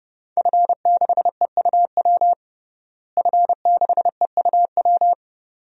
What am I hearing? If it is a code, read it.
Morse code, heard as a keyed tone, about 32 wpm, 700 Hz. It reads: F6EUW F6EUW
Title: DXCC-100 Call Signs - 5 Characters - Repeated 2x - 30wpm